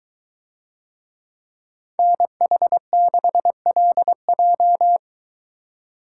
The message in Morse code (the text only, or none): NH6LJ